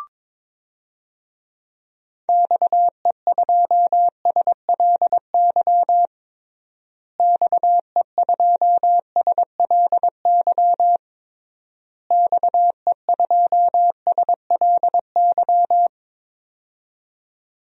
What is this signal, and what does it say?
Morse code, heard as a keyed tone, about 22 wpm, 700 Hz. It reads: XE2SLY XE2SLY XE2SLY
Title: DXCC-100 Call Signs - 6 Characters - Repeated 3x - 22wpm